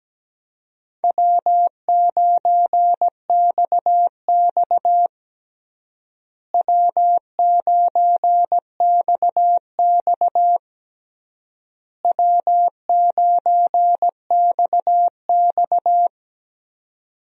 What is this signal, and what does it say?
Morse code, heard as a keyed tone, about 17 wpm, 700 Hz. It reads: W9XX W9XX W9XX
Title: USA Call Signs - 4 Characters - Repeated 3x - 17wpm